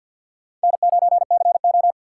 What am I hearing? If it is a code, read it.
Morse code, heard as a keyed tone, about 50 wpm, 700 Hz. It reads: N9CO
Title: USA Call Signs - 4 Characters - 2x Character Spacing - 50wpm